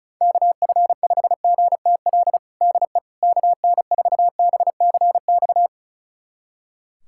Morse code, heard as a keyed tone, about 34 wpm, 700 Hz. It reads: KF5GTL DE KN4BCX
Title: USA Callsign DE Callsign 35wpm